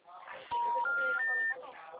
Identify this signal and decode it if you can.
Special information tone, with speech.